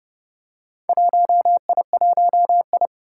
Morse code, heard as a keyed tone, about 29 wpm, 700 Hz. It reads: 1I1I
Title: Four Challenging Letter-Number Combinations 30wpm